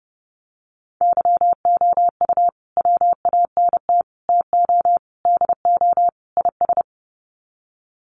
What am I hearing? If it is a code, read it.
Morse code, heard as a keyed tone, about 30 wpm, 700 Hz. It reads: YOU WANT TO DO IS
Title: Sets of 5 Words Encore2 30wpm